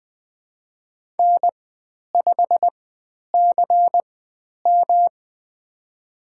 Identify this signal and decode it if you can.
Morse code, heard as a keyed tone, about 20 wpm, spaced as for 10 wpm, 700 Hz. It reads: N5CM
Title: 109 Lesson 18g - Letter m - Call Signs 20wpm